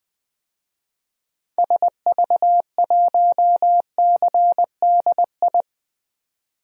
Morse code, heard as a keyed tone, about 20 wpm, 700 Hz. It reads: SV1CDI
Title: DXCC-100 Call Signs - 6 Characters - 20wpm